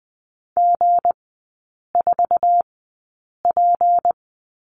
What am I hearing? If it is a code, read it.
Morse code, heard as a keyed tone, about 20 wpm, spaced as for 8 wpm, 700 Hz. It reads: G4P
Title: 159 Lesson 26e - Letter g - Three Characters 20wpm